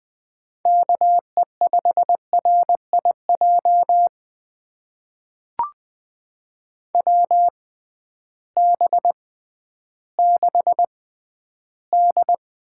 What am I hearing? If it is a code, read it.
Morse code, heard as a keyed tone, about 20 wpm, spaced as for 12 wpm, 700 Hz. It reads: KE5RIJ WB6D
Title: USA Call Signs - 6 Characters - 6x Character Spacing - 20wpm